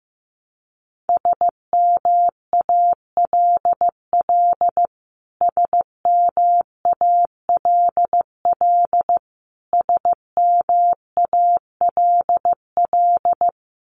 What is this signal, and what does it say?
Morse code, heard as a keyed tone, about 15 wpm, 700 Hz. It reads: SMALL SMALL SMALL
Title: Top 200 Words Repeated 15wpm